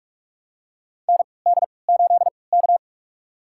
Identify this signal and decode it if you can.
Morse code, heard as a keyed tone, about 46 wpm, spaced as for 26 wpm, 700 Hz. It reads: ND8K
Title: USA Call Signs - 3x Character Spacing - 45wpm